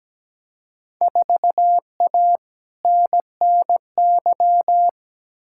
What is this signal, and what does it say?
Morse code, heard as a keyed tone, about 17 wpm, 700 Hz. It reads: 4A NNY
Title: Field Day Exchanges Encore - 17wpm